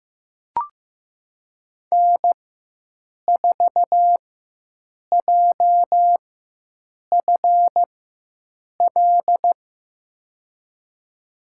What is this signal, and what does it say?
Morse code, heard as a keyed tone, about 15 wpm, spaced as for 7 wpm, 700 Hz. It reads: N4JFL
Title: USA Call Signs - 5 Characters - 4x Character Spacing - 15wpm